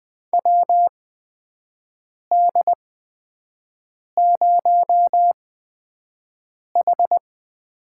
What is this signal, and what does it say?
Morse code, heard as a keyed tone, about 20 wpm, spaced as for 5 wpm, 700 Hz. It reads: WD0H
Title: USA Call Signs - 4 Characters - 8x Character Spacing - 20wpm